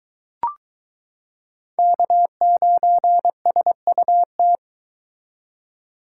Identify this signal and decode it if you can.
Morse code, heard as a keyed tone, about 22 wpm, 700 Hz. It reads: K9SUT